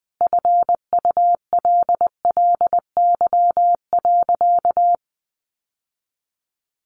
Morse code, heard as a keyed tone, about 19 wpm, 700 Hz. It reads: FULLY.
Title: Sentences from Top 1300 Words 20wpm